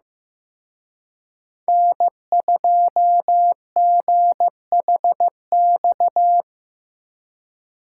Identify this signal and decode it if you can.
Morse code, heard as a keyed tone, about 19 wpm, 700 Hz. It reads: N2GHX
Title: Speed Racing - USA Call Signs 15wpm